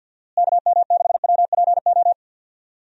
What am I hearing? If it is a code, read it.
Morse code, heard as a keyed tone, about 50 wpm, 700 Hz. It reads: KM6WPO
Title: USA Call Signs - 6 Characters - Repeated 2x - 50wpm